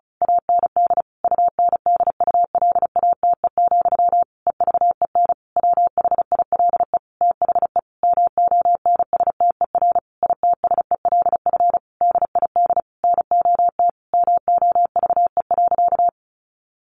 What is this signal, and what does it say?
Morse code, heard as a keyed tone, about 34 wpm, 700 Hz. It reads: AND UNDULATE, EVEN WHILE THE MONSTER ITSELF DID NOT MOVE.
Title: A Sound Of Thunder 35wpm